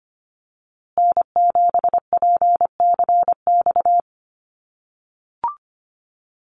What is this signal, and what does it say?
Morse code, heard as a keyed tone, about 25 wpm, 700 Hz. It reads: N7PCX
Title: USA Call Signs Repeated 3x 25wpm